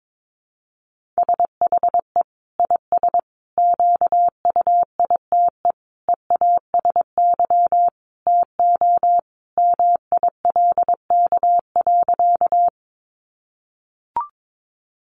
Morse code, heard as a keyed tone, about 22 wpm, 700 Hz. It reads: SHE IS QUITE EASY TO MILK.
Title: Sentences from Top 1500 Words 22wpm